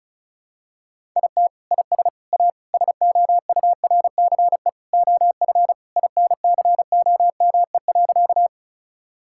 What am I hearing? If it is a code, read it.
Morse code, heard as a keyed tone, about 34 wpm, 700 Hz. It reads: IT IS A SOURCE OF INCOME.
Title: Sentences from Top 1000 Words - Review Part 5 - 35wpm